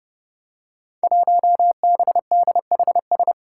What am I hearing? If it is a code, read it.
Morse code, heard as a keyed tone, about 30 wpm, 700 Hz. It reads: 1BDHS
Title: Five Challenging Letter-Number Combinations 30wpm